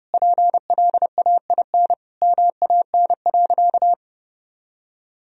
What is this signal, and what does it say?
Morse code, heard as a keyed tone, about 32 wpm, 700 Hz. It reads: PLAIN MAN.
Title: Sentences from Top 1000 Words - Review Part 2 - 30wpm